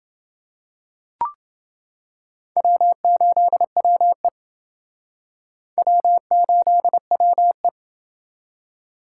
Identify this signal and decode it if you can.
Morse code, heard as a keyed tone, about 28 wpm, 700 Hz. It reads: W8WE W8WE